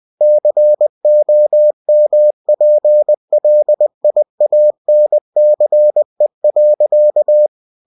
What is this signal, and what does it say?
Morse code, heard as a keyed tone, about 20 wpm, 600 Hz. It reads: COMPLIANCE.
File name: cw-books / winnie-the-pooh / winnie-20wpm-page-0121